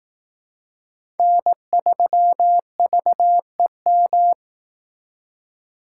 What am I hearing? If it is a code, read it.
Morse code, heard as a keyed tone, about 18 wpm, 700 Hz. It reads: N3VEM
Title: Speed Racing - USA 1x3 Call Signs 15wpm